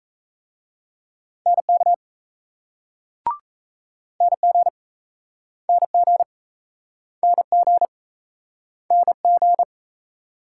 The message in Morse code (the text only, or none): NK NG NG NG NG